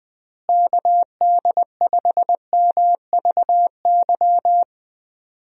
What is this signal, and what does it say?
Morse code, heard as a keyed tone, about 20 wpm, 700 Hz. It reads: KD5MVY
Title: USA Call Signs - 6 Characters - Repeated 2x - 20wpm